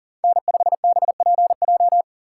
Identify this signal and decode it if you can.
Morse code, heard as a keyed tone, about 40 wpm, 700 Hz. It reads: N5BPJ
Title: USA Call Signs - 5 Characters - 6x Character Spacing - 40wpm